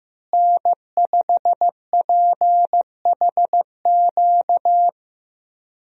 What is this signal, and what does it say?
Morse code, heard as a keyed tone, about 15 wpm, 700 Hz. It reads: N5PHQ
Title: USA Call Signs - 5 Characters - Repeated 2x - 15wpm